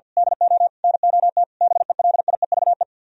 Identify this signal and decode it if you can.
Morse code, heard as a keyed tone, about 48 wpm, 700 Hz. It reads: DO NOT BELIEVE
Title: Speed-Racing - Sets of 4 Words 50wpm